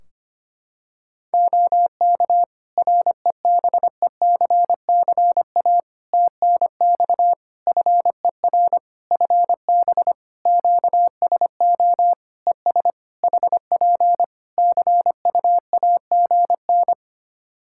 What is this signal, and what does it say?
Morse code, heard as a keyed tone, about 25 wpm, 700 Hz. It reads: OK REBECCA TNX FER FB QSO ES HP CUAGN
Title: Practice QSO - Exchange 4 - Ending - 25wpm